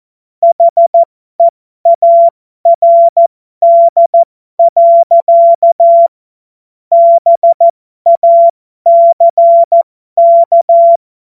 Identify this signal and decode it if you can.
Morse code, heard as a keyed tone, about 14 wpm, spaced as for 12 wpm, 680 Hz. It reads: HEARD. BACK
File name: Morsekurs / Witze